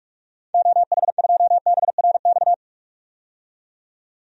Morse code, heard as a keyed tone, about 41 wpm, 700 Hz. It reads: OH2BRX
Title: DXCC-100 Call Signs - 6 Characters - 45wpm